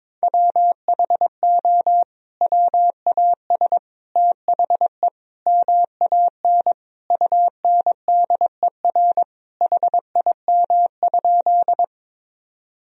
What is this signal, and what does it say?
Morse code, heard as a keyed tone, about 22 wpm, 700 Hz. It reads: WHO WAS THE MAN UNDER HIM?
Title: Sentences from Top 1000 Words - Review Part 1 - 22wpm